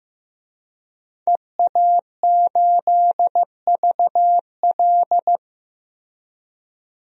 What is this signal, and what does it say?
Morse code, heard as a keyed tone, about 15 wpm, 700 Hz. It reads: EA8VL
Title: DXCC-100 Call Signs - 5 Characters - Repeated 3x - 15wpm